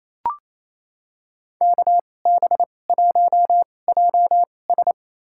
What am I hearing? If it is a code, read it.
Morse code, heard as a keyed tone, about 28 wpm, spaced as for 20 wpm, 700 Hz. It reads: KB1JS
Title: USA Call Signs - 2x Character Spacing - 28wpm